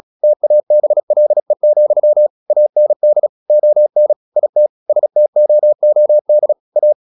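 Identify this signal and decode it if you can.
Morse code, heard as a keyed tone, about 36 wpm, 600 Hz. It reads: TABLE, AND ON IT STOOD A
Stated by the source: Morse practice at 36 wpm, text of Ken Follett's novel Never, from CW med Gnister.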